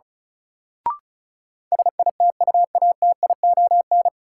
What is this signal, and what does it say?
Morse code, heard as a keyed tone, about 34 wpm, 700 Hz. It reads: SITUATION
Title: Sentences from Top 1200 Words - New Word Drill - 35wpm